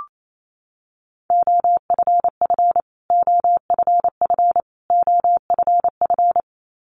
Morse code, heard as a keyed tone, about 28 wpm, 700 Hz. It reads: OFF OFF OFF
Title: Top 200 Words Repeated 28wpm